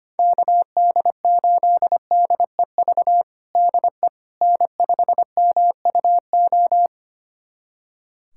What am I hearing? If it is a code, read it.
Morse code, heard as a keyed tone, about 25 wpm, 700 Hz. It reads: KD8DEV DE N5MUO
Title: USA Callsign DE Callsign 25wpm